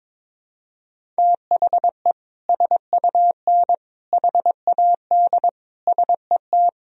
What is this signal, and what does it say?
Morse code, heard as a keyed tone, about 22 wpm, 700 Hz. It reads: THE SUN HAD SET
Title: Sentences from Top 300 Words and Abbreviations 22wpm